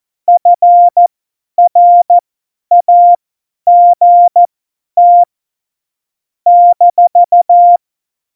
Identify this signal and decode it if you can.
Morse code, heard as a keyed tone, about 14 wpm, spaced as for 10 wpm, 700 Hz. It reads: FRAGT -
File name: Morsekurs / Witze / Witz 15. Dezember 2020, 50 F70